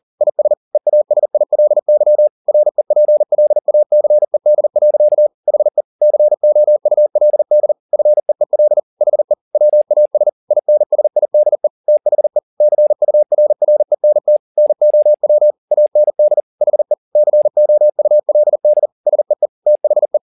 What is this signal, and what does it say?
Morse code, heard as a keyed tone, about 40 wpm, 600 Hz. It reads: IS EASILY REPLACED. HE COULD FEEL HE WAS INSIDE THE CURRENT NOW AND HE COULD SEE THE
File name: the old man and the sea 2 40wpm